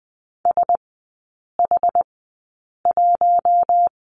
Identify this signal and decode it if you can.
Morse code, heard as a keyed tone, about 20 wpm, spaced as for 8 wpm, 700 Hz. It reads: SH1
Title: 107 Lesson 18e - Letter m - Three Characters 20wpm